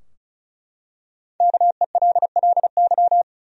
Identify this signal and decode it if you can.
Morse code, heard as a keyed tone, about 36 wpm, 700 Hz. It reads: KELLY
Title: USA Common Names - Max 5 Letters - 35wpm